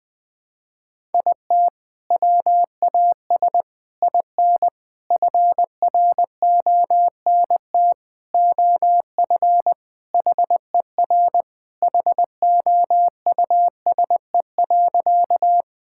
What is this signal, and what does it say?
Morse code, heard as a keyed tone, about 20 wpm, 700 Hz. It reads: IT WAS IN FRONT OF HER HOUSE.
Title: Sentences with 4-Word Phrases - Spoken Beginning - 20wpm